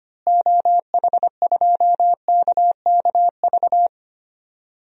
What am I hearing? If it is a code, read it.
Morse code, heard as a keyed tone, about 25 wpm, 700 Hz. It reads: OH2KKV
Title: DXCC-100 Call Signs - 6 Characters - Repeated 3x - 25wpm